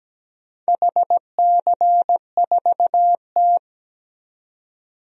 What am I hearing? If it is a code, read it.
Morse code, heard as a keyed tone, about 17 wpm, 700 Hz. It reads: HC4T
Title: DXCC-100 Call Signs - 4 Characters - 17wpm